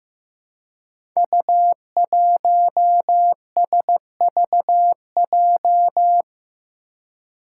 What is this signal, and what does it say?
Morse code, heard as a keyed tone, about 15 wpm, 700 Hz. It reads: U1SVJ
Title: Five Challenging Letter-Number Combinations 15wpm